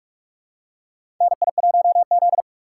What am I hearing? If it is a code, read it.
Morse code, heard as a keyed tone, about 43 wpm, 700 Hz. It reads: NI1Z
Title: USA Call Signs - 4 Characters - 6x Character Spacing - 45wpm